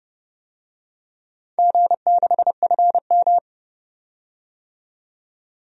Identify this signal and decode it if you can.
Morse code, heard as a keyed tone, about 31 wpm, 700 Hz. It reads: G6FM
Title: DXCC-100 Call Signs - 4 Characters - 30wpm